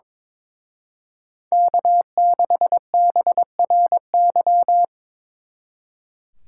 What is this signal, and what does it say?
Morse code, heard as a keyed tone, about 25 wpm, 700 Hz. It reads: K6BRY
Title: Speed Racing - USA 1x3 Call Signs 22wpm